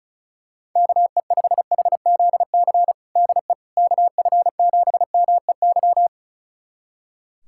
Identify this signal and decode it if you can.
Morse code, heard as a keyed tone, about 34 wpm, 700 Hz. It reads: KE5HZC DE KF7MEY
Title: USA Callsign DE Callsign - 6 Characters - 35wpm